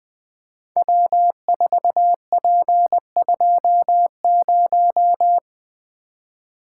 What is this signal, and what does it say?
Morse code, heard as a keyed tone, about 20 wpm, 700 Hz. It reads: W4P20
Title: Code Groups - 5 Letter:Numbers - 20wpm